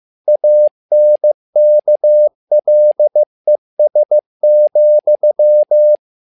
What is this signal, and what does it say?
Morse code, heard as a keyed tone, about 15 wpm, 600 Hz. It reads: ANKLES,
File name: learning center / tower trouble / tower trouble 15wpm